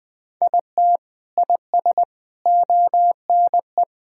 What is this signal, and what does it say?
Morse code, heard as a keyed tone, about 20 wpm, 700 Hz. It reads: IT IS ONE
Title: Sentences with 4-Word Phrases - Spoken Beginning - 20wpm